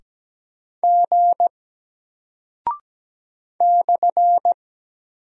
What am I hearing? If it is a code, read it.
Morse code, heard as a keyed tone, about 17 wpm, 700 Hz. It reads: G /
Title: Single Character (Letter:Number:Punctuation:Prosign) 17wpm